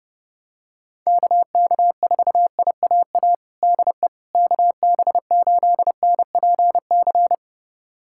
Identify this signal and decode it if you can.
Morse code, heard as a keyed tone, about 32 wpm, 690 Hz. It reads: KK4IAA DE KB8NPC
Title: USA Callsign DE Callsign - 6 Characters - 30wpm